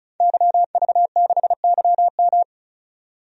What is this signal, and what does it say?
Morse code, heard as a keyed tone, about 36 wpm, 700 Hz. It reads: YV6YM
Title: DXCC-100 Call Signs - 5 Characters - 35wpm